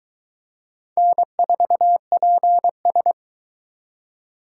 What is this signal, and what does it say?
Morse code, heard as a keyed tone, about 22 wpm, 700 Hz. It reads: N4PS